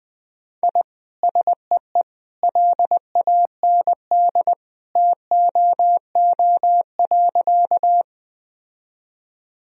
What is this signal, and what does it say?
Morse code, heard as a keyed tone, about 20 wpm, 700 Hz. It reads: I SEE LAND TOO.
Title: Sentences from Top 1000 Words - Review Part 1 - 20wpm